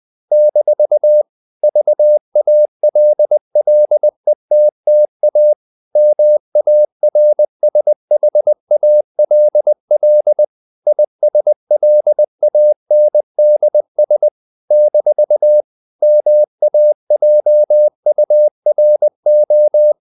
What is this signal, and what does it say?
Morse code, heard as a keyed tone, about 20 wpm, 600 Hz. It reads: - VALLETTA MARSHALL ISLANDS - MAJURO
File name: CW / Practice / NationalCapitals20